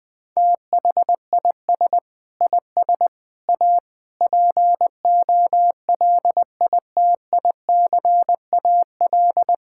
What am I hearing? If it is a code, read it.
Morse code, heard as a keyed tone, about 21 wpm, 700 Hz. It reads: THIS IS A POLITICAL
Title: Sentences from Top 1000 Words - Review Part 6 - 20wpm